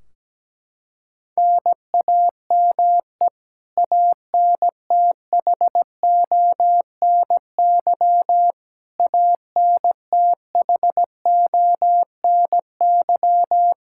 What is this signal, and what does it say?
Morse code, heard as a keyed tone, about 17 wpm, 700 Hz. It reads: NAME ANTHONY ANTHONY
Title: Practice QSO - Exchange 1 - Beginning - 17wpm